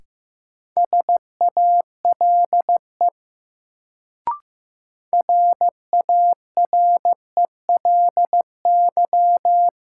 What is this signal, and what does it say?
Morse code, heard as a keyed tone, about 16 wpm, 700 Hz. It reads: SALE RARELY
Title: Sentences from Top 1400 Words - New Word Drill Alt - 15wpm